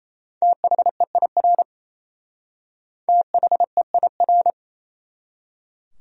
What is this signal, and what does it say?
Morse code, heard as a keyed tone, about 33 wpm, 700 Hz. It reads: THEIR THEIR
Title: Speed-Racing Top 100 Words 28wpm